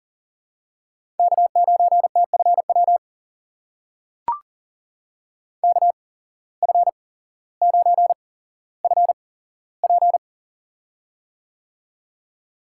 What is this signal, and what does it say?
Morse code, heard as a keyed tone, about 40 wpm, spaced as for 17 wpm, 700 Hz. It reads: K9TFW KF9FP
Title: USA Call Signs - 5 Characters - 8x Character Spacing - 40wpm